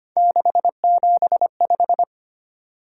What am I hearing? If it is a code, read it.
Morse code, heard as a keyed tone, about 25 wpm, 700 Hz. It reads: 675
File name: Three Letter-Number Combinations - 25wpm